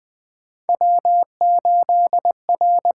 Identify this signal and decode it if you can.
Morse code, heard as a keyed tone, about 20 wpm, 700 Hz. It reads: W8R
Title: USA Call Signs - 4 Characters - 8x Character Spacing - 20wpm